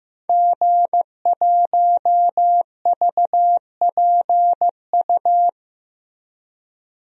Morse code, heard as a keyed tone, about 15 wpm, 700 Hz. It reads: G1VPU
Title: DXCC-100 Call Signs - 5 Characters - Repeated 3x - 15wpm